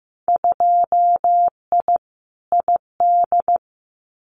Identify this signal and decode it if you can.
Morse code, heard as a keyed tone, about 15 wpm, 700 Hz. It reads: 2I ID
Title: Winter Field Day Exchanges Encore 15wpm